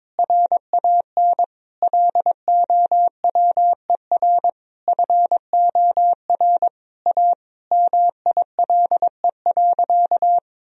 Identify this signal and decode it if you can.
Morse code, heard as a keyed tone, about 22 wpm, 700 Hz. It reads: RAN LOWER FOR A MILE.
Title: Sentences from Top 1000 Words - Review Part 4 - 22wpm